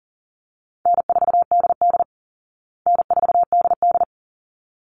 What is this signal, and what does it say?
Morse code, heard as a keyed tone, about 40 wpm, 700 Hz. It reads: N4DD N4DD
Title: Call Signs Repeated 3x - 40wpm